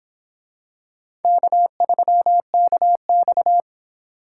Morse code, heard as a keyed tone, about 25 wpm, 700 Hz. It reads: K3KX